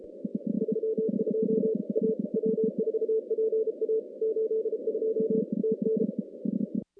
Morse code, heard as a keyed tone, about 32 wpm, 410 Hz, with noise. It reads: 32 A W4PA 82 TN
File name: contests / CW-SS-k5tr- -2100